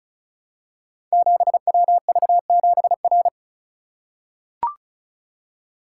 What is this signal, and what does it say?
Morse code, heard as a keyed tone, about 36 wpm, 700 Hz. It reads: 7WV7R
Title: Code Groups - 5 Letter:Numbers - 35wpm